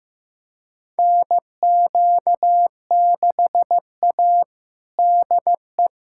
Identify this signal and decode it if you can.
Morse code, heard as a keyed tone, about 15 wpm, 700 Hz. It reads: NQ6A DE
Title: USA Callsign DE Callsign - 4 Characters - 15wpm